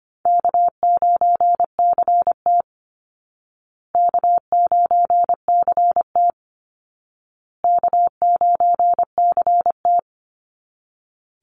Morse code, heard as a keyed tone, about 25 wpm, 700 Hz. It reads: K9CT K9CT K9CT
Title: Easy Call Signs Repeated 3x - 25wpm